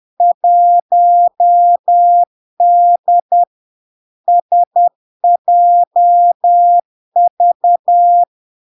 Morse code, heard as a keyed tone, about 10 wpm, 700 Hz. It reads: 1D SJV
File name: LARK-FD-2022-Log-in-CW-10wpm-700-Hz